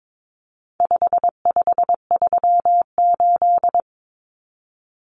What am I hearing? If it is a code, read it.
Morse code, heard as a keyed tone, about 22 wpm, 700 Hz. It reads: 5538
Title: Common Cut Numbers - 4-Digit 22wpm